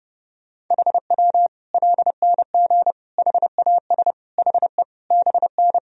Morse code, heard as a keyed tone, about 30 wpm, 700 Hz. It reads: HW LNG HAS HE BN
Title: Sentences from Top 100 Words and Abbreviations 30wpm